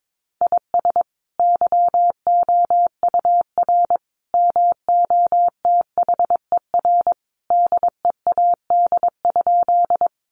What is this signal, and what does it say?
Morse code, heard as a keyed tone, about 22 wpm, 700 Hz. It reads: IS YOUR MOTHER DEAD?
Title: Sentences from Top 1500 Words 22wpm